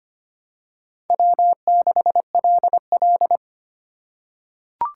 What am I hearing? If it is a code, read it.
Morse code, heard as a keyed tone, about 25 wpm, 700 Hz. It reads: W6LL
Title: USA Call Signs - 6x Character Spacing - 25wpm